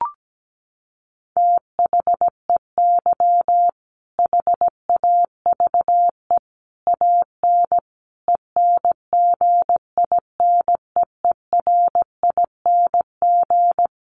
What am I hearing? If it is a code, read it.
Morse code, heard as a keyed tone, about 17 wpm, 700 Hz. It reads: THEY HAVE AN ENGINEERING
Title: Sentences from Top 1500 Words 17wpm